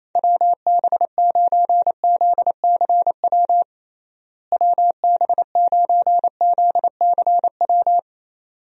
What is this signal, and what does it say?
Morse code, heard as a keyed tone, about 28 wpm, 700 Hz. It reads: WB9ZCW WB9ZCW
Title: USA Call Signs - 6 Characters - Repeated 2x - 28wpm